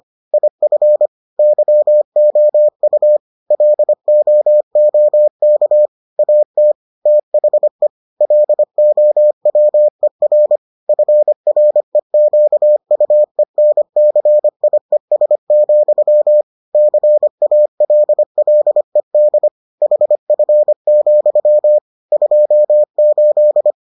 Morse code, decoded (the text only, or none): IF YOU LOOK AT THE LOWER FREQUENCIES, CALLED HF, 28